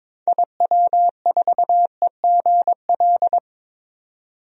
Morse code, heard as a keyed tone, about 21 wpm, 700 Hz. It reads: IW4EGL
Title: DXCC-100 Call Signs - 6 Characters - 22wpm